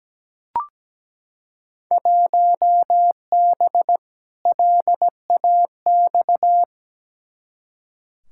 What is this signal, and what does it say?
Morse code, heard as a keyed tone, about 17 wpm, 700 Hz. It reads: 1B LAX
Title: Field Day Exchanges - 17wpm